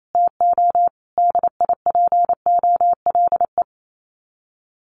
Morse code, heard as a keyed tone, about 28 wpm, 700 Hz. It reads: TO DIPOLE
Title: Common QSO Phrases 28wpm